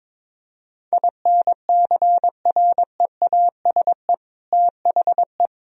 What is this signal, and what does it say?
Morse code, heard as a keyed tone, about 22 wpm, 700 Hz. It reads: INCREASE THE
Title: Sentences from Top 1000 Words - Review Part 4 - 22wpm